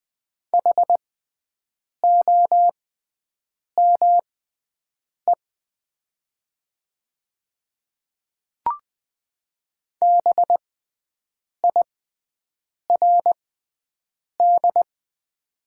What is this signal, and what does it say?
Morse code, heard as a keyed tone, about 20 wpm, spaced as for 7 wpm, 700 Hz. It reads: HOME BIRD
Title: 6x Character Spacing - Top 500 Words - 20wpm